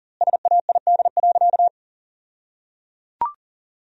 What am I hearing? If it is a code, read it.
Morse code, heard as a keyed tone, about 37 wpm, 700 Hz. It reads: SAID.
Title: 2x Word-Spacing - Sentences from Top 100 Words - 40wpm